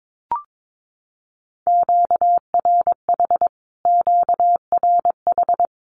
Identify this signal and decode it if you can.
Morse code, heard as a keyed tone, about 22 wpm, 700 Hz. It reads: QRH QRH
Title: Q-Codes Repeated 3x 22wpm